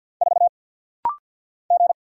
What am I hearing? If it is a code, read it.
Morse code, heard as a keyed tone, about 50 wpm, 700 Hz. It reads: 4 G
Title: Single Letter-Number Alt - Rapid-Fire 50wpm